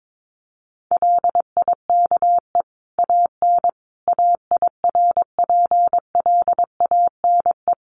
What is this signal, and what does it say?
Morse code, heard as a keyed tone, about 22 wpm, 700 Hz. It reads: LIKE AN AIRPLANE
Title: A Sound Of Thunder 22wpm